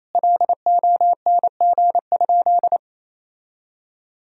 Morse code, heard as a keyed tone, about 27 wpm, 700 Hz. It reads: LONG?
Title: Sentences from Top 1000 Words - Review Part 3 - 28wpm